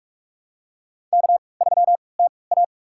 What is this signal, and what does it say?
Morse code, heard as a keyed tone, about 46 wpm, spaced as for 25 wpm, 700 Hz. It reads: K3TA
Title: USA Call Signs - 4 Characters - 3x Character Spacing - 45wpm